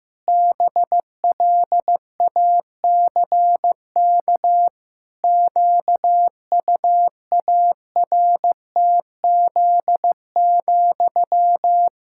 Morse code, heard as a keyed tone, about 16 wpm, 700 Hz. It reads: BLACK QUARTZ,
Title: Pangram Challenge v2 - 15wpm